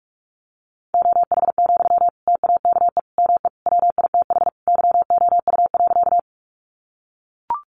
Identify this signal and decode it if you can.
Morse code, heard as a keyed tone, about 44 wpm, 700 Hz. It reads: OH, TAKE ME WITH YOU.
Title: Sentences from Top 500 Words 45wpm